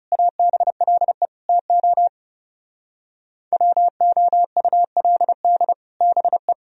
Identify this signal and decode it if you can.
Morse code, heard as a keyed tone, about 36 wpm, 690 Hz. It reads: ABLE TO WOULD BE
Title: Speed-Racing - Sets of 4 Words 30wpm